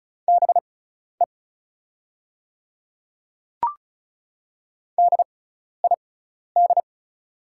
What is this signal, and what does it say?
Morse code, heard as a keyed tone, about 36 wpm, spaced as for 12 wpm, 690 Hz. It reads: BE DID
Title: 6x Character Spacing - Top 100 Words - 35wpm